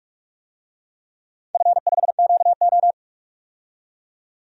Morse code, heard as a keyed tone, about 46 wpm, 700 Hz. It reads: F5QO
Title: DXCC-100 Call Signs - 4 Characters - 45wpm